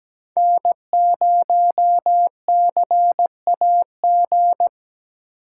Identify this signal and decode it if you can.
Morse code, heard as a keyed tone, about 17 wpm, 700 Hz. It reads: N0CAG
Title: USA Call Signs - 5 Characters - 2x Character Spacing - 17wpm